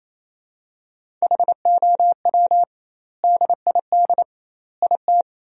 Morse code, heard as a keyed tone, about 27 wpm, 700 Hz. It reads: HOW DID IT
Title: 2x Word-Spacing - Sentences from Top 300 Words - 28wpm